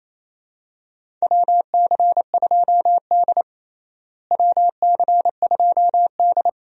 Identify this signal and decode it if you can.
Morse code, heard as a keyed tone, about 27 wpm, 700 Hz. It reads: WC2D WC2D
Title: USA Call Signs - 4 Characters - Repeated 3x - 28wpm